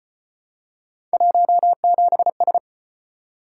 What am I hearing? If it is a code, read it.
Morse code, heard as a keyed tone, about 34 wpm, 700 Hz. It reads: 17S